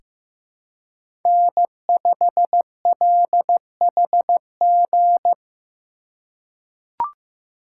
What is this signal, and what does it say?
Morse code, heard as a keyed tone, about 15 wpm, 700 Hz. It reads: N5LHG
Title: USA Call Signs - 5 Characters - 2x Character Spacing - 15wpm